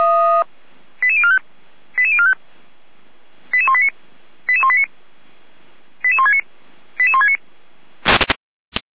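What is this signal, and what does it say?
Five-tone selective call (ZVEI), heard as quick runs of five tones, tones of about 70 ms. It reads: 88045 88045 88189 88189 88169 88169
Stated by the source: ZVEI five-tone selective call received by SDR.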